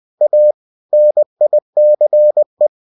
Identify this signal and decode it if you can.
Morse code, heard as a keyed tone, about 20 wpm, 600 Hz. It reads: A NICE